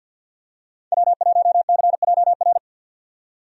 Morse code, heard as a keyed tone, about 50 wpm, 700 Hz. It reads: W1CJR
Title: USA Call Signs - 5 Characters - 50wpm